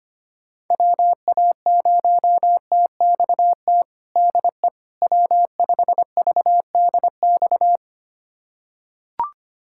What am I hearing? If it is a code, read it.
Morse code, heard as a keyed tone, about 25 wpm, 700 Hz. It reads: WA0TXT DE W5VDX
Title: USA Callsign DE Callsign 25wpm